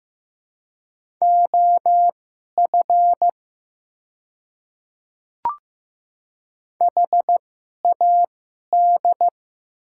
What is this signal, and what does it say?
Morse code, heard as a keyed tone, about 15 wpm, spaced as for 11 wpm, 700 Hz. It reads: OF HAD
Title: 2x Character Spacing - Top 100 Words - 15wpm